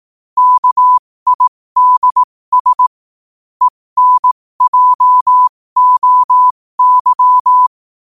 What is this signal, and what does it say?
Morse code, heard as a keyed tone, about 18 wpm, spaced as for 15 wpm, 1000 Hz. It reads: KIDS ENJOY
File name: qotdfiles / 15wpm